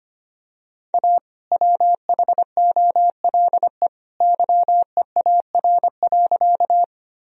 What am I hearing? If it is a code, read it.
Morse code, heard as a keyed tone, about 25 wpm, 700 Hz. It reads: A WHOLE YEAR.
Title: Sentences from Top 1000 Words - Review Part 1 - 25wpm